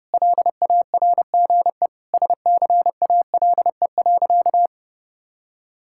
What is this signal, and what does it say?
Morse code, heard as a keyed tone, about 30 wpm, 700 Hz. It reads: LARGE SCALE.
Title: Sentences from Top 1000 Words - Review Part 5 - 30wpm